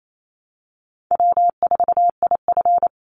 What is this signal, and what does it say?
Morse code, heard as a keyed tone, about 28 wpm, 700 Hz. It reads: W4IF
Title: Call Signs Repeated 2x - 28wpm